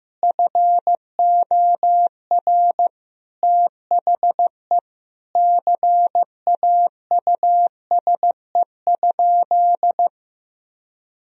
Morse code, heard as a keyed tone, about 15 wpm, 700 Hz. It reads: FOR THE CAUSE?
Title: Sentences from Top 1000 Words - Review Part 4 - 15wpm